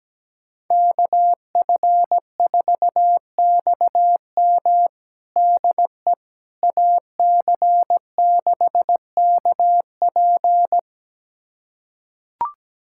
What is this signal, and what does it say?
Morse code, heard as a keyed tone, about 17 wpm, 700 Hz. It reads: KF4XM DE AC6KP
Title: USA Callsign DE Callsign - 5 Characters - 17wpm